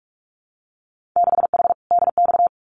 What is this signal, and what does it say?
Morse code, heard as a keyed tone, about 44 wpm, 700 Hz. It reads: 6H DX
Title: Winter Field Day Exchanges Encore 45wpm